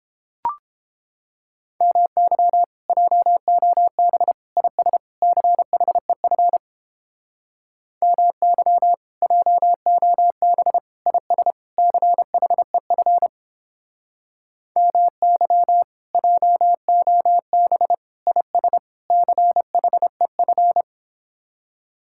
Morse code, heard as a keyed tone, about 29 wpm, 700 Hz. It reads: MY JOB IS CHEF MY JOB IS CHEF MY JOB IS CHEF